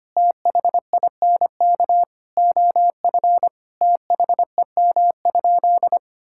Morse code, heard as a keyed tone, about 25 wpm, 700 Hz. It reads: THINK OF THEM?
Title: Sentences from Top 1600 Words 25wpm